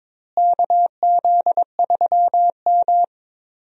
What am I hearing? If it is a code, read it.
Morse code, heard as a keyed tone, about 22 wpm, 700 Hz. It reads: KZ3M
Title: USA Call Signs - 4 Characters - Repeated 2x - 22wpm